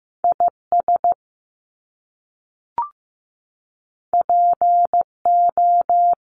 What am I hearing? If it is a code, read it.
Morse code, heard as a keyed tone, about 15 wpm, 700 Hz. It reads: IS PO
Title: Top Two Letter Combinations in Words 15wpm